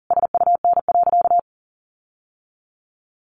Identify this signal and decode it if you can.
Morse code, heard as a keyed tone, about 40 wpm, 700 Hz. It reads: SUN.
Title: Sentences from Top 1300 Words 40wpm